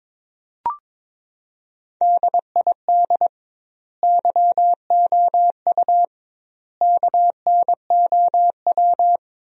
Morse code, heard as a keyed tone, about 23 wpm, 700 Hz. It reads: DID YOU KNOW
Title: 2x Word-Spacing - Sentences from Top 100 Words - 22wpm